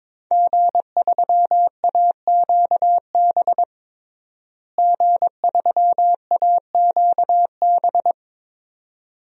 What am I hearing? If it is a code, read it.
Morse code, heard as a keyed tone, about 22 wpm, 700 Hz. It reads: G3AQB G3AQB
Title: DXCC-100 Call Signs - 5 Characters - Repeated 2x - 22wpm